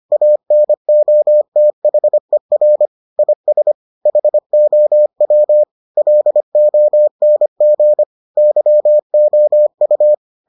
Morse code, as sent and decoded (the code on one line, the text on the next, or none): .- -. --- - .... .-. / .. ... / .... --- .-- / .-.. --- -. --. / -.-- --- ..-
ANOTHER IS HOW LONG YOU